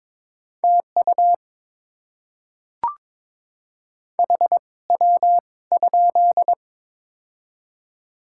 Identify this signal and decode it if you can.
Morse code, heard as a keyed tone, about 21 wpm, spaced as for 17 wpm, 700 Hz. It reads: TU HW?
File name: Top 20 QSO Elements - 2x Character Spacing - 22wpm